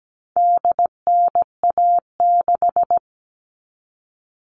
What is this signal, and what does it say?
Morse code, heard as a keyed tone, about 17 wpm, 700 Hz. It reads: DNA6
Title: All Cut Numbers - 4-Digit 17wpm